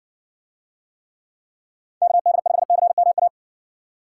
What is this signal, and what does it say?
Morse code, heard as a keyed tone, about 60 wpm, 700 Hz. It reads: KD5CMA